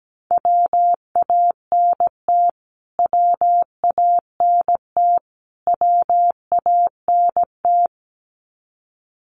Morse code, heard as a keyed tone, about 17 wpm, 700 Hz. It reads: WANT WANT WANT
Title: Top 500 Words Repeated 17wpm